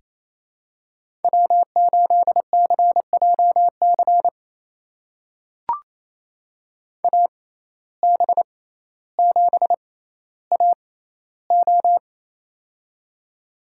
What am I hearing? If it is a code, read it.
Morse code, heard as a keyed tone, about 28 wpm, spaced as for 14 wpm, 700 Hz. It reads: W8CJC AB7AO
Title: USA Call Signs - 5 Characters - 6x Character Spacing - 28wpm